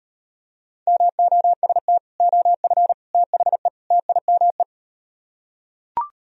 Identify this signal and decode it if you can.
Morse code, heard as a keyed tone, about 38 wpm, 700 Hz. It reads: MOST OF THE TIME